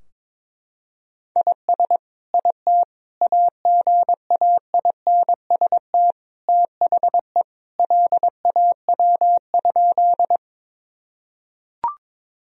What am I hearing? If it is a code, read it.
Morse code, heard as a keyed tone, about 22 wpm, 700 Hz. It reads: IS IT AGAINST THE LAW?
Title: Sentences from Top 1000 Words - Review Part 3 - 22wpm